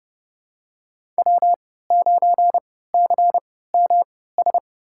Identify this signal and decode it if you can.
Morse code, heard as a keyed tone, about 30 wpm, spaced as for 17 wpm, 700 Hz. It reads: W9CMS
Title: USA Call Signs - 5 Characters - 3x Character Spacing - 30wpm